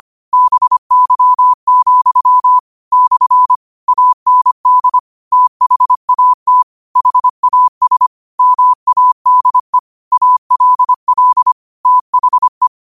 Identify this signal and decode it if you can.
Morse code, heard as a keyed tone, about 25 wpm, 1000 Hz. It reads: BY, / AND THAT HAS MADE ALL THE
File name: qotdfiles / 25wpm